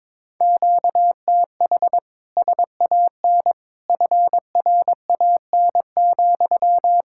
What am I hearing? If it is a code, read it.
Morse code, heard as a keyed tone, about 22 wpm, 700 Hz. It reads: QTH SAN FRAN,
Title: Speed-Racing Common QSO Phrases - 22wpm